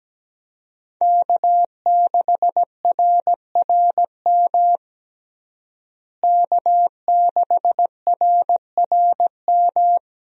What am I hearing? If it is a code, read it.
Morse code, heard as a keyed tone, about 17 wpm, 700 Hz. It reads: K6RRM K6RRM
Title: USA Call Signs - 5 Characters - Repeated 2x - 17wpm